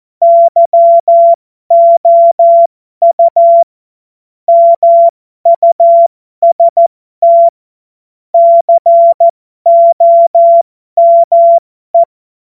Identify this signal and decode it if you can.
Morse code, heard as a keyed tone, about 14 wpm, spaced as for 12 wpm, 680 Hz. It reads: YOU MUST COME
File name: Morsekurs / Witze